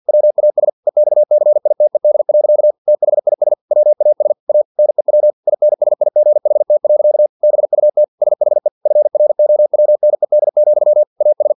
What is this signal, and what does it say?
Morse code, heard as a keyed tone, about 50 wpm, 600 Hz. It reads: WAS EXCITED. THIS WAS A NEW INSIGHT. BUT SHE FROWNED, AS
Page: Never 49wpm